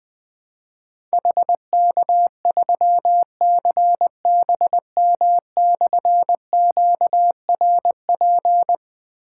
Morse code, heard as a keyed tone, about 21 wpm, 700 Hz. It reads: HK3CBM/QRP
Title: Difficult Call Signs 20wpm